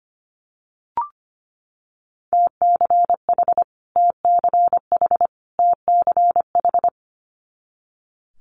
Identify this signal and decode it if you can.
Morse code, heard as a keyed tone, about 25 wpm, 700 Hz. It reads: TCH TCH TCH
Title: Multi-letter Phonemes Repeated 3x 25wpm